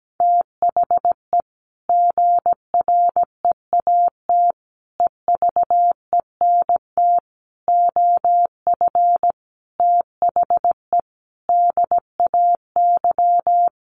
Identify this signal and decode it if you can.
Morse code, heard as a keyed tone, about 17 wpm, 700 Hz. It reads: THE GREAT EVENT OF THE DAY
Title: Sentences from Top 1500 Words 17wpm